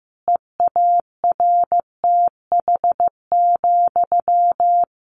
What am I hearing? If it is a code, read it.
Morse code, heard as a keyed tone, about 15 wpm, 700 Hz. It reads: EARTH,
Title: A Sound Of Thunder 15wpm